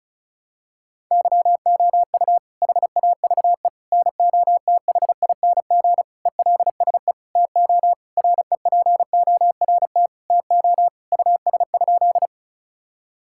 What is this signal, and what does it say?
Morse code, heard as a keyed tone, about 34 wpm, 700 Hz. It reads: YOU HAVE NOTHING ELSE TO REPORT TO US?
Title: Sentences from Top 1000 Words - Review Part 6 - 35wpm